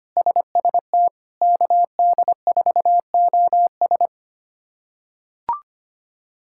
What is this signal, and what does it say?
Morse code, heard as a keyed tone, about 25 wpm, 700 Hz. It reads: SST KD4OS
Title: K1USN SST QSO Practice - 25wpm